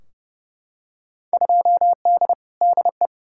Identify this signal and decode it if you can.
Morse code, heard as a keyed tone, about 32 wpm, 700 Hz. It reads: 2D DE
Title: Field Day Exchanges Encore - 30wpm